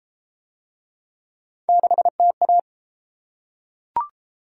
Morse code, heard as a keyed tone, about 30 wpm, 700 Hz. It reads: 6TA